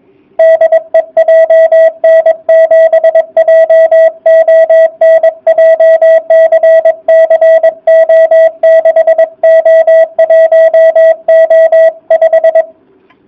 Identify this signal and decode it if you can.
Morse code, heard as a keyed tone, about 22 wpm, 660 Hz. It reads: DE JN7JON JCC O6O1O5